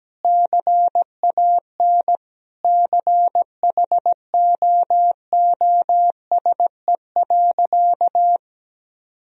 Morse code, heard as a keyed tone, about 17 wpm, 700 Hz. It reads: CAN CHOOSE.
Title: Sentences from Top 1000 Words - Review Part 4 - 17wpm